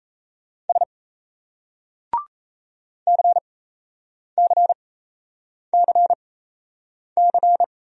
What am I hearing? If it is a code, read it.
Morse code, heard as a keyed tone, about 35 wpm, spaced as for 8 wpm, 700 Hz. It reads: S CCCC
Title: Speed-Racing Single Letters 28wpm